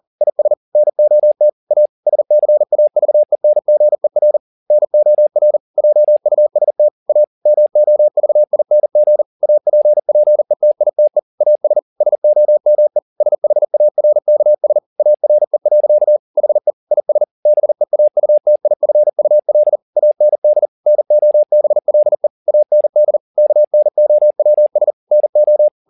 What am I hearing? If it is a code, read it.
Morse code, heard as a keyed tone, about 40 wpm, 600 Hz. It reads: IS NOT A SCAVENGER NOR JUST A MOVING APPETITE AS SOME SHARKS ARE. HE IS BEAUTIFUL AND NOBLE AND KNOWS NO
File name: the old man and the sea 2 40wpm